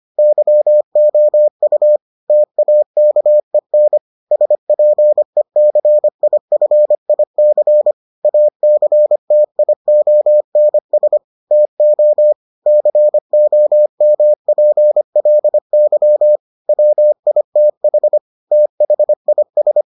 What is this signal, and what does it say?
Morse code, heard as a keyed tone, about 25 wpm, 600 Hz. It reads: YOU TAKEN SPECIFIC ACTIONS TO COMPLY WITH THIS